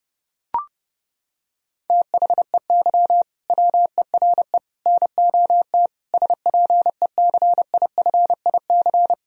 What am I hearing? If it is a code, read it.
Morse code, heard as a keyed tone, about 30 wpm, 700 Hz. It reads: THEY WERE NOT SPECIFIC
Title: Sentences from Top 1000 Words - Review Part 3 - 30wpm